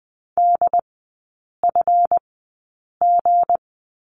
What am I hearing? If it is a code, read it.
Morse code, heard as a keyed tone, about 20 wpm, spaced as for 8 wpm, 700 Hz. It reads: DFG
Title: 191 Lesson 31e - Letter v - Three Characters 20wpm